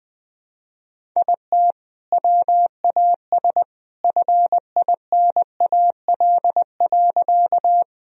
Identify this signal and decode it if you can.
Morse code, heard as a keyed tone, about 21 wpm, 700 Hz. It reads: IT WAS FINAL.
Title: Sentences from Top 1000 Words - Review Part 2 - 20wpm